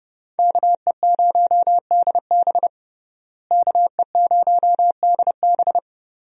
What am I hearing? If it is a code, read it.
Morse code, heard as a keyed tone, about 30 wpm, 700 Hz. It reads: KE0DB KE0DB
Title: USA Call Signs - 5 Characters - Repeated 3x - 30wpm